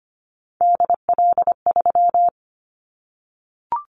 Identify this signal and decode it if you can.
Morse code, heard as a keyed tone, about 25 wpm, 700 Hz. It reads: DL3
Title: Callsign Prefixes for Top 25 DXCC - Repeated 3x - 25wpm